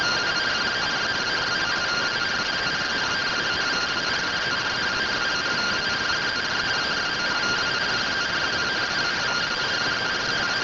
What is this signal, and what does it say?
Two-tone teleprinter signal, about 170 Hz shift.